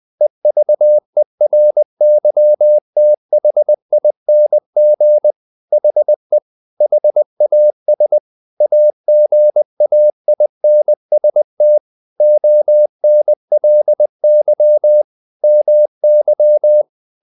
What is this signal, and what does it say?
Morse code, heard as a keyed tone, about 20 wpm, 600 Hz. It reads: EVERYTHING HE HAS AGAINST ONLY MY
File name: the old man and the sea / the old man and the sea part 1 20wpm